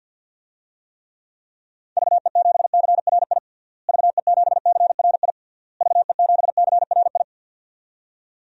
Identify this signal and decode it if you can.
Morse code, heard as a keyed tone, about 50 wpm, 700 Hz. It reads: VE7CRI VE7CRI VE7CRI
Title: DXCC-100 Call Signs - 6 Characters - Repeated 3x - 50wpm